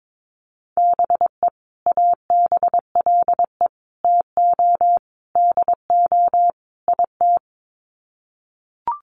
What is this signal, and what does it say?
Morse code, heard as a keyed tone, about 22 wpm, 700 Hz. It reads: BE ABLE TO DO IT
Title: Sets of 5 Words Encore2 22wpm